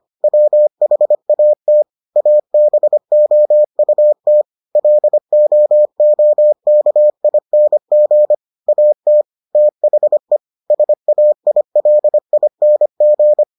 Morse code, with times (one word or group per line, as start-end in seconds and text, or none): 0.2-1.8 WHAT
2.2-4.4 ABOUT
4.8-8.3 LOOKING
8.7-9.2 AT
9.6-10.4 THE
10.7-13.4 SAILING